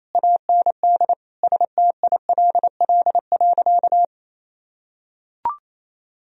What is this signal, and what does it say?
Morse code, heard as a keyed tone, about 28 wpm, 700 Hz. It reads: AND STILL.
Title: Sentences from Top 1000 Words - Review Part 2 - 28wpm